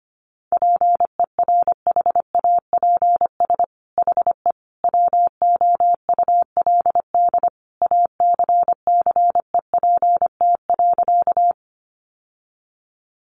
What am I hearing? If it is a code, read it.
Morse code, heard as a keyed tone, about 25 wpm, 700 Hz. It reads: PERHAPS HE WOULD ACCEPT.
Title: Sentences from Top 1300 Words 25wpm